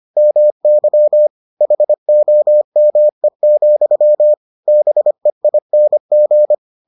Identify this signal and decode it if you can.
Morse code, heard as a keyed tone, about 25 wpm, 600 Hz. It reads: MY HOME, BEING